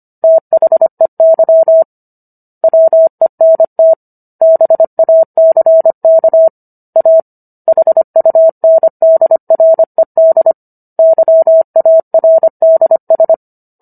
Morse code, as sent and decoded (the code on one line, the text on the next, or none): - .... -.-- / .-- . -. - / -... .- -.-. -.- / .- / .... ..- -. -.. .-. -.. / -.-- .- .-. -.. ...
THEY WENT BACK A HUNDRED YARDS